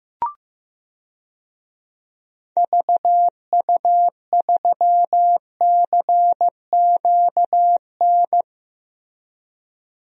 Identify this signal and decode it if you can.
Morse code, heard as a keyed tone, about 15 wpm, 700 Hz. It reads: VU3CQN
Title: DXCC-100 Call Signs - 6 Characters - Repeated 3x - 15wpm